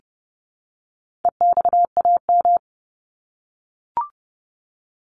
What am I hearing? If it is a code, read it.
Morse code, heard as a keyed tone, about 30 wpm, 700 Hz. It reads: EXAM
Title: Top 501-750 Words 30wpm